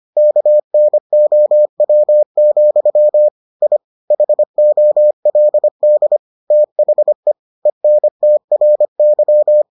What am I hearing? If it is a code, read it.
Morse code, heard as a keyed tone, about 25 wpm, 600 Hz. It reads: KNOW, I HOLD THE ENTRY